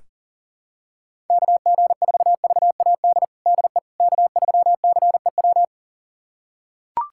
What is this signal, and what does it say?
Morse code, heard as a keyed tone, about 40 wpm, 700 Hz. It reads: KG4VAD DE K3CEW
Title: USA Callsign DE Callsign 40wpm